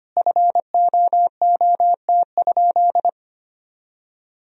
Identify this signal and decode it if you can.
Morse code, heard as a keyed tone, about 25 wpm, 700 Hz. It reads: FOOT?
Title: Sentences from Top 500 Words and Abbreviations 25wpm